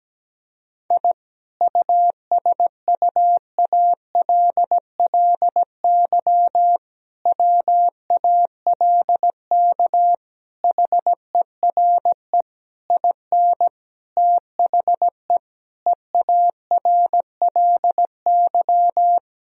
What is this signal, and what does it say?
Morse code, heard as a keyed tone, about 17 wpm, 700 Hz. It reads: I USUALLY WALK HERE IN THE EARLY
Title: Sentences from Top 1000 Words - Review Part 3 - 17wpm